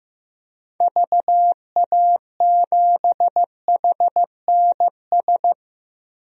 Morse code, heard as a keyed tone, about 15 wpm, 700 Hz. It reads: VA7HNS
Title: DXCC-100 Call Signs - 6 Characters - 15wpm